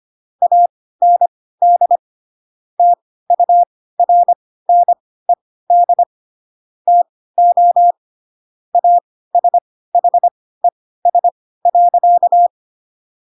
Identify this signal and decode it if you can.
Morse code, heard as a keyed tone, about 25 wpm, spaced as for 16 wpm, 700 Hz. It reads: AND TURNED TO ASHES.
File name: Pinocchio / Pinocchio ch6 10 25 ik1hge